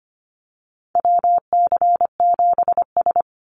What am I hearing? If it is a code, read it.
Morse code, heard as a keyed tone, about 25 wpm, 700 Hz. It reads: WC7S
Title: Call Signs - 25wpm